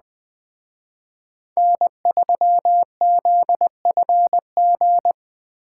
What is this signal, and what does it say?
Morse code, heard as a keyed tone, about 20 wpm, 700 Hz. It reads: N3ZFG
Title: Speed Racing - USA 1x3 Call Signs 17wpm